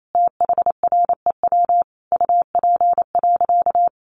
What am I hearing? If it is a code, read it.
Morse code, heard as a keyed tone, about 28 wpm, 700 Hz. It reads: THREW UP.
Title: A Sound Of Thunder 28wpm